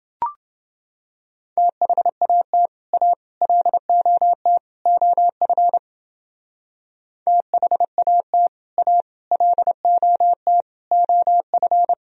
Morse code, heard as a keyed tone, about 29 wpm, 700 Hz. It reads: THAT A LOT OF THAT A LOT OF